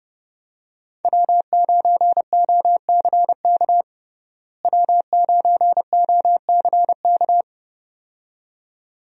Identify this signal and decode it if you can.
Morse code, heard as a keyed tone, about 30 wpm, 700 Hz. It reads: W9OCK W9OCK
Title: USA Call Signs - 5 Characters - Repeated 2x - 30wpm